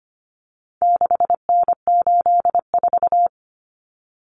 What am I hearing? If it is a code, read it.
Morse code, heard as a keyed tone, about 25 wpm, 700 Hz. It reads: 6N84
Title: Common Cut Numbers - 4-Digit 25wpm